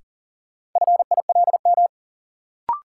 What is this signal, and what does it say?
Morse code, heard as a keyed tone, about 40 wpm, 700 Hz. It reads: FILM
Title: Sentences from Top 1000 Words - Review Part 3 - New Word Drill - 40wpm